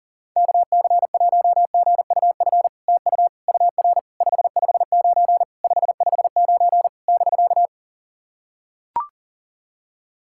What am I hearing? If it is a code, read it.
Morse code, heard as a keyed tone, about 40 wpm, 700 Hz. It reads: KC1GUF TU UR 559 559 <BK>
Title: SOTA QSO Practice 40wpm